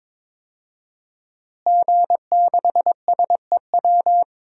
Morse code, heard as a keyed tone, about 22 wpm, 700 Hz. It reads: G6SEW
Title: DXCC-100 Call Signs - 5 Characters - Repeated 2x - 22wpm